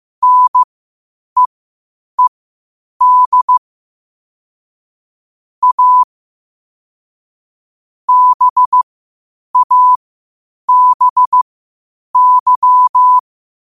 Morse code, heard as a keyed tone, about 15 wpm, spaced as for 8 wpm, 1000 Hz. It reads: NEED A BABY
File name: qotdfiles / 8wpm